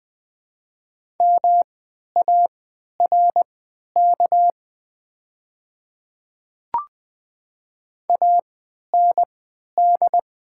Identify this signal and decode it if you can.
Morse code, heard as a keyed tone, about 20 wpm, spaced as for 11 wpm, 700 Hz. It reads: MARK AND
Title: 3x Character Spacing - Top 500 Words - 20wpm